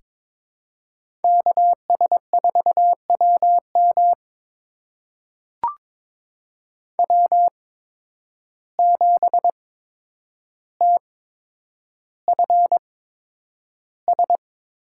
Morse code, heard as a keyed tone, about 21 wpm, spaced as for 9 wpm, 700 Hz. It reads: KS4WM W7TFS
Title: USA Call Signs - 5 Characters - 8x Character Spacing - 22wpm